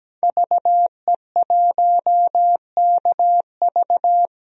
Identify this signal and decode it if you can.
Morse code, heard as a keyed tone, about 17 wpm, 700 Hz. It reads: VE1KV
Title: DXCC-100 Call Signs - 5 Characters - Repeated 2x - 17wpm